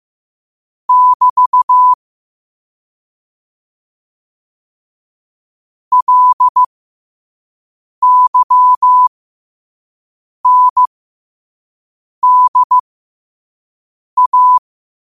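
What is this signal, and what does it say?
Morse code, heard as a keyed tone, about 15 wpm, spaced as for 5 wpm, 1000 Hz. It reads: <BT> LYNDA
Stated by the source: Morse quote of the day at 5 wpm.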